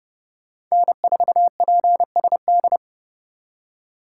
Morse code, heard as a keyed tone, about 37 wpm, 700 Hz. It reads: N4PSD
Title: Speed Racing - USA 1x3 Call Signs 30wpm